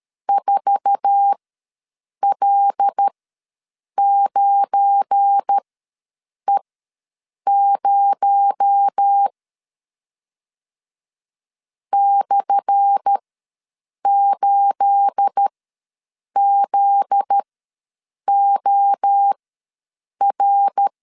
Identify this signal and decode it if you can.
Morse code, heard as a keyed tone, about 13 wpm, spaced as for 7 wpm, 800 Hz. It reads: 4L9E0 /8ZOR
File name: LEZIONI / 10 SIGNS / 10A SLASH ws 12wpm 3443